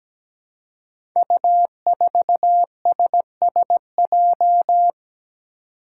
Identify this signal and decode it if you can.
Morse code, heard as a keyed tone, about 17 wpm, 700 Hz. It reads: U4SSJ
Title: Five Challenging Letter-Number Combinations 17wpm